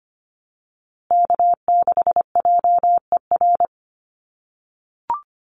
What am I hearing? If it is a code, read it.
Morse code, heard as a keyed tone, about 25 wpm, 700 Hz. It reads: K6JER
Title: USA Call Signs Repeated 3x 25wpm